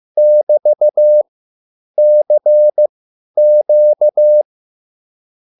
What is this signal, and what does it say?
Morse code, heard as a keyed tone, about 15 wpm, spaced as for 9 wpm, 600 Hz. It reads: <BT>CQ